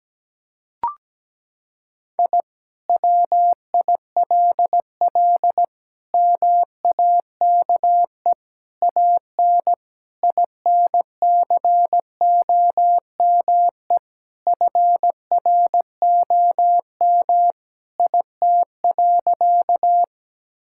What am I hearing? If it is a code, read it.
Morse code, heard as a keyed tone, about 17 wpm, 700 Hz. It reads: I WILL MAKE AN INCOME FROM IT.
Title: Sentences from Top 1000 Words - Review Part 5 - 17wpm